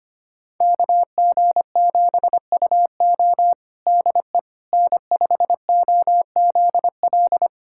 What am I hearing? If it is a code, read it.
Morse code, heard as a keyed tone, about 25 wpm, 700 Hz. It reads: KG7UO DE N5OZL
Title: USA Callsign DE Callsign - 5 Characters - 25wpm